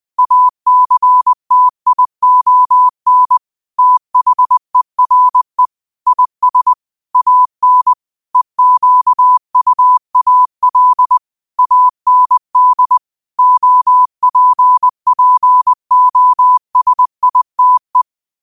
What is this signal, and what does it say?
Morse code, heard as a keyed tone, about 20 wpm, 1000 Hz. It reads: ACTION THERE IS AN EQUAL AND OPPOSITE
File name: qotdfiles / 20wpm